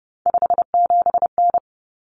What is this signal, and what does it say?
Morse code, heard as a keyed tone, about 30 wpm, 700 Hz. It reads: 57N
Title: Common Cut Numbers - 3-Digit 30wpm